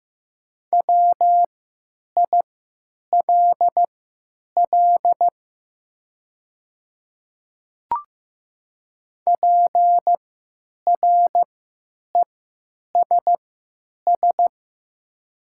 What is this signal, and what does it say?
Morse code, heard as a keyed tone, about 15 wpm, spaced as for 9 wpm, 700 Hz. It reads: WILL PRESS
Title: 3x Character Spacing - Top 500 Words - 15wpm